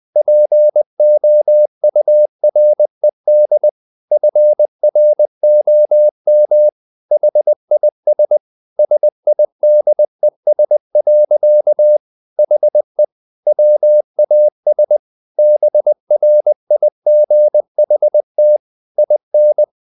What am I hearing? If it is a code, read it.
Morse code, heard as a keyed tone, about 20 wpm, 600 Hz. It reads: POURED FROM HIS SIDES. HE WAS BRIGHT IN
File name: the old man and the sea / the old man and the sea part 1 20wpm